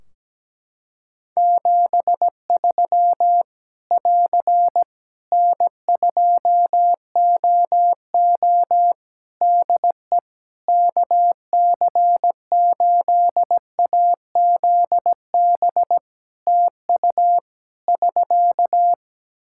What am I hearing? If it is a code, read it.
Morse code, heard as a keyed tone, about 17 wpm, 700 Hz. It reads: 73 <AR> N2OO DE KC8AZB TU <SK>
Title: Practice QSO - Exchange 4 - Ending - 17wpm